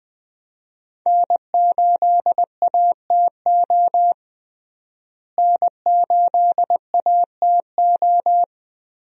Morse code, heard as a keyed tone, about 19 wpm, 700 Hz. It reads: N8ATO N8ATO
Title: USA Call Signs - 5 Characters - Repeated 3x - 20wpm